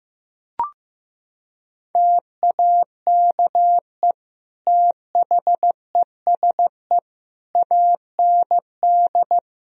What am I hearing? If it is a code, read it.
Morse code, heard as a keyed tone, about 15 wpm, 700 Hz. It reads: TAKE THESE AND
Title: Sentences from Top 1000 Words - Review Part 1 - 15wpm